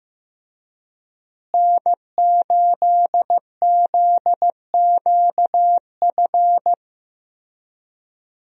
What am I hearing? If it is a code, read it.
Morse code, heard as a keyed tone, about 15 wpm, 700 Hz. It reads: N8ZQF
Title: USA Call Signs - 5 Characters - Repeated 3x - 15wpm